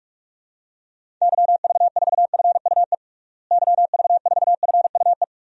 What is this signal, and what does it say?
Morse code, heard as a keyed tone, about 48 wpm, 700 Hz. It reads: YV4FUE YV4FUE
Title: DXCC-100 Call Signs - 6 Characters - Repeated 2x - 45wpm